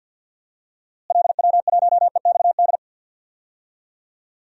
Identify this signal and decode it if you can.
Morse code, heard as a keyed tone, about 52 wpm, 700 Hz. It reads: LW1EXD
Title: DXCC-100 Call Signs - 6 Characters - Repeated 2x - 50wpm